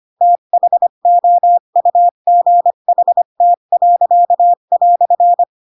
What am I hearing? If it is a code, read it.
Morse code, heard as a keyed tone, about 25 wpm, 700 Hz. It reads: THOUGHT."
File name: Pinocchio / Pinocchio ch3 20 25 ik1hge